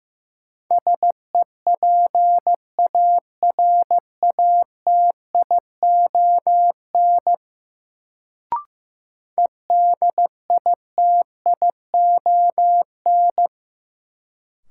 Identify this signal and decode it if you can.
Morse code, heard as a keyed tone, about 15 wpm, 700 Hz. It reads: SEPARATION EDITION
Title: Sets of 5 Words - New Word Drill - 15wpm